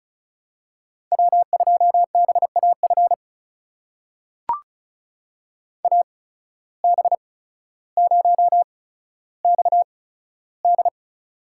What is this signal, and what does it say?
Morse code, heard as a keyed tone, about 36 wpm, spaced as for 15 wpm, 700 Hz. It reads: W2BAF AB0XD
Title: USA Call Signs - 5 Characters - 8x Character Spacing - 35wpm